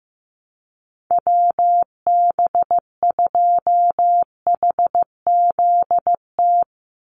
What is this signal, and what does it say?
Morse code, heard as a keyed tone, about 15 wpm, 700 Hz. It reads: WB2HZT
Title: USA Call Signs 15wpm